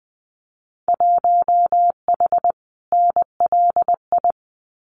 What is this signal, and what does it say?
Morse code, heard as a keyed tone, about 20 wpm, 700 Hz. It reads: 1H NLI
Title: Winter Field Day Exchanges 20wpm